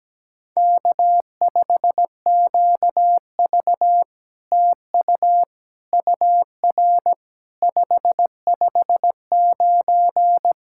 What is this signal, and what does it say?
Morse code, heard as a keyed tone, about 18 wpm, 700 Hz. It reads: K5QV TU UR 559
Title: SOTA QSO Practice 17wpm